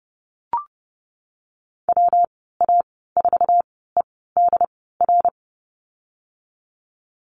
Morse code, heard as a keyed tone, about 32 wpm, spaced as for 17 wpm, 700 Hz. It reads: WA4EDR
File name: Lesson 37g - Letter x - Call Signs 30wpm